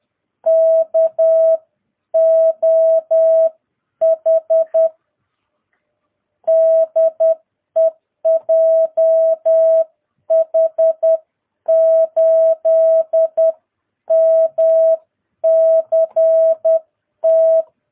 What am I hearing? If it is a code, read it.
Morse code, heard as a keyed tone, about 10 wpm, spaced as for 9 wpm, 660 Hz. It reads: KOH DE JH8MCT
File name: jh8cbh / 2022cw